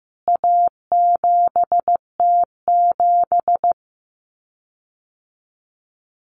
Morse code, heard as a keyed tone, about 15 wpm, 700 Hz. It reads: A7T7
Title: CWT Exchange 15wpm